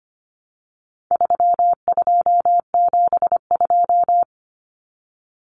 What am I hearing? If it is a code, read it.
Morse code, heard as a keyed tone, about 25 wpm, 700 Hz. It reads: 3272
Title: Common Cut Numbers - 4-Digit 25wpm